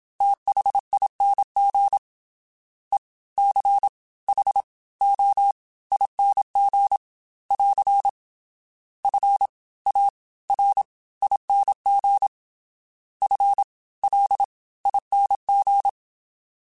Morse code, heard as a keyed tone, about 27 wpm, spaced as for 20 wpm, 800 Hz. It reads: THING ECHOING <AR> FARING FLING
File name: Special INGs / Special INGs-mp3s / Fast ing @25